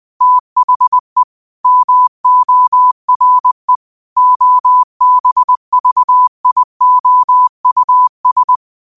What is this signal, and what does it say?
Morse code, heard as a keyed tone, about 20 wpm, 1000 Hz. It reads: THE MORE OBVIOUS
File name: qotdfiles / 20wpm